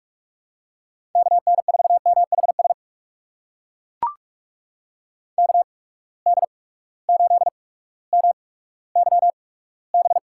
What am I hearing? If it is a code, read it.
Morse code, heard as a keyed tone, about 42 wpm, spaced as for 19 wpm, 700 Hz. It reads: KN4MHS KD8MYB
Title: USA Call Signs - 6 Characters - 8x Character Spacing - 45wpm